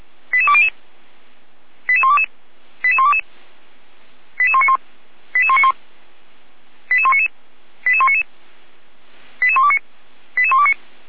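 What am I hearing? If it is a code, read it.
Five-tone selective call (ZVEI), heard as quick runs of five tones, tones of about 70 ms. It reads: 88200 88122 88122 88191 88191 88199 88199 88129 88129
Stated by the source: Five-tone ZVEI selective call with code 88314, received by SDR.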